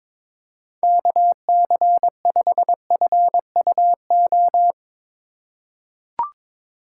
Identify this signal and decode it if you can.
Morse code, heard as a keyed tone, about 22 wpm, 700 Hz. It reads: KC5FUO
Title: USA Call Signs - 4x Character Spacing - 22wpm